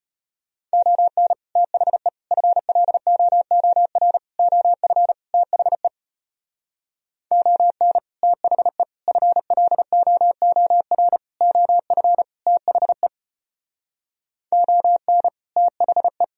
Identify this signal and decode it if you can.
Morse code, heard as a keyed tone, about 32 wpm, 700 Hz. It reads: ON THE FLOOR OF THE ON THE FLOOR OF THE ON THE